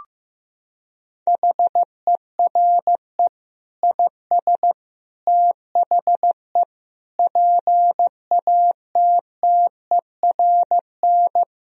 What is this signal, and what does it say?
Morse code, heard as a keyed tone, about 15 wpm, 700 Hz. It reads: HERE IS THE PATTERN
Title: Sentences from Top 1000 Words - Review Part 2 - 15wpm